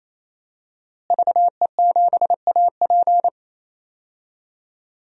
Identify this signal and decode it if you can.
Morse code, heard as a keyed tone, about 28 wpm, 700 Hz. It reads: VE7AP
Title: DXCC-100 Call Signs - 5 Characters - Repeated 3x - 28wpm